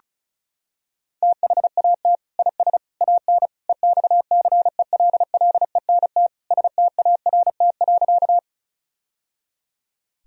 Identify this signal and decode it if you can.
Morse code, heard as a keyed tone, about 36 wpm, 700 Hz. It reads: THAT IS AN EXCELLENT START.
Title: Sentences from Top 1000 Words - Review Part 1 - 35wpm